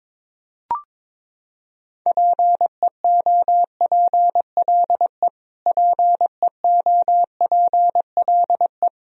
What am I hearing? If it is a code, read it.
Morse code, heard as a keyed tone, about 22 wpm, 700 Hz. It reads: PEOPLE PEOPLE
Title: Top 100 Words Repeated 22wpm